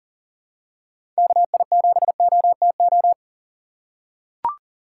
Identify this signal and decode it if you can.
Morse code, heard as a keyed tone, about 38 wpm, 700 Hz. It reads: KI7OTO
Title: USA Call Signs - 6 Characters - 4x Character Spacing - 40wpm